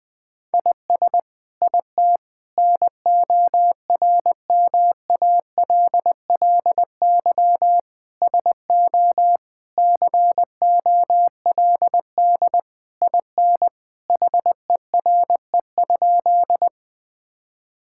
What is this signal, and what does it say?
Morse code, heard as a keyed tone, about 21 wpm, 700 Hz. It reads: IS IT NORMALLY SO COLD IN HERE?
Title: Sentences from Top 1000 Words - Review Part 6 - 20wpm